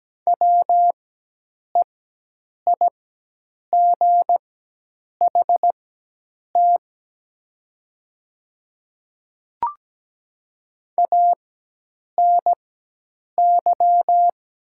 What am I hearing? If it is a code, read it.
Morse code, heard as a keyed tone, about 17 wpm, spaced as for 8 wpm, 700 Hz. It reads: WEIGHT ANY
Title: 4x Character Spacing - Top 500 Words - 17wpm